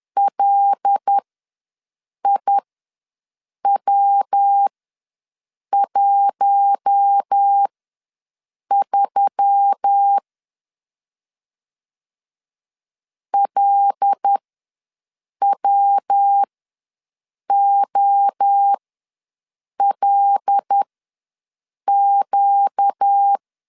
Morse code, heard as a keyed tone, about 10 wpm, spaced as for 6 wpm, 800 Hz. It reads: LIW13 LWOLQ
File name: LEZIONI / 5 LBVC / 5 L ws 10wpm 3636